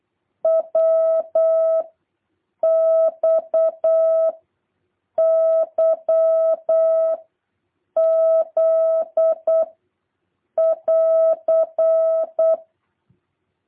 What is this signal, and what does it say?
Morse code, heard as a keyed tone, about 8 wpm, 650 Hz. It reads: W X Y Z <AR>
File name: jh8cbh / 2023cw